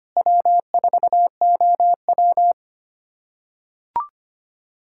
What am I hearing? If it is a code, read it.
Morse code, heard as a keyed tone, about 25 wpm, 700 Hz. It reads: W4OW
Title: USA Callsign DE Callsign - 4 Characters - 25wpm